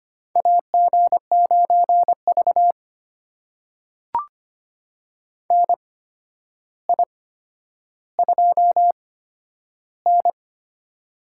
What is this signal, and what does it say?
Morse code, heard as a keyed tone, about 25 wpm, spaced as for 11 wpm, 700 Hz. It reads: AG9V NI2N
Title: USA Call Signs - 4 Characters - 8x Character Spacing - 25wpm